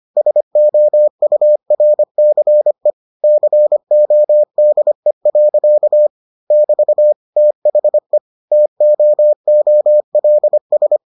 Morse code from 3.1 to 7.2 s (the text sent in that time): CODE. <BT>